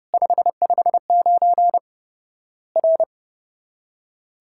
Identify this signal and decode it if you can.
Morse code, heard as a keyed tone, about 30 wpm, 700 Hz. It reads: 559 R
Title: Worked All Europe DX Contest - QTC Exchange - 30wpm